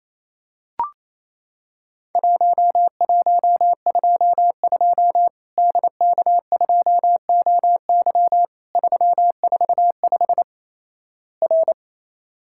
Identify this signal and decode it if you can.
Morse code, heard as a keyed tone, about 28 wpm, 700 Hz. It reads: 1122 DK2OY 345 R
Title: Worked All Europe DX Contest - QTC Exchange - 28wpm